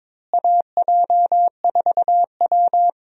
Morse code, heard as a keyed tone, about 21 wpm, 700 Hz. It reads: AJ4W
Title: USA Call Signs - 4 Characters - Repeated 2x - 22wpm